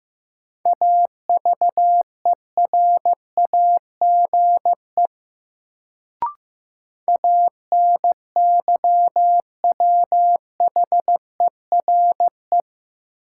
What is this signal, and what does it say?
Morse code, heard as a keyed tone, about 15 wpm, 700 Hz. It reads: AVERAGE ANYWHERE
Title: Sentences from Top 1400 Words - New Word Drill Alt - 15wpm